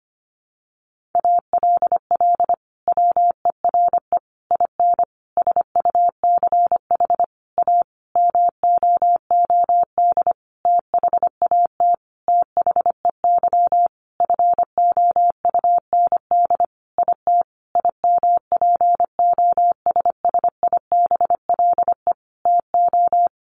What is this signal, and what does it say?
Morse code, heard as a keyed tone, about 25 wpm, 700 Hz. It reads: ALL WERE IN SUCH A MOOD THAT THEY FOUND IT IMPOSSIBLE TO
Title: Sentences from Top 1100 Words 25wpm